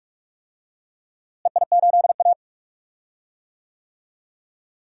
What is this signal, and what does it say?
Morse code, heard as a keyed tone, about 45 wpm, 700 Hz. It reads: EI8A
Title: DXCC-100 Call Signs - 4 Characters - Repeated 2x - 45wpm